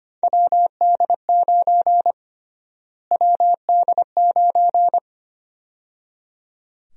Morse code, heard as a keyed tone, about 25 wpm, 700 Hz. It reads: WD9 WD9
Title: USA Call Sign Prefixes - Repeated 2x - 25wpm